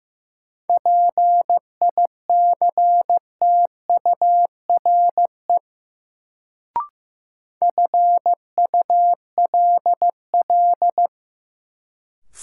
Morse code, heard as a keyed tone, about 16 wpm, 700 Hz. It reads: PICTURE FULL
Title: Sentences from Top 500 Words - New Word Drill - 15wpm